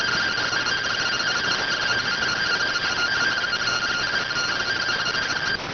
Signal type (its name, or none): two-tone teleprinter signal